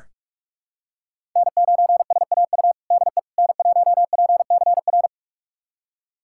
Morse code, heard as a keyed tone, about 41 wpm, 700 Hz. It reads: N9SAU DE N1PCR
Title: USA Callsign DE Callsign - 5 Characters - 45wpm